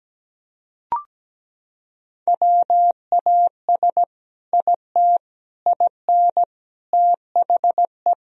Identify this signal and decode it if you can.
Morse code, heard as a keyed tone, about 17 wpm, 700 Hz. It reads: WAS IT IN THE
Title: Sentences from Top 1000 Words - Review Part 4 - 17wpm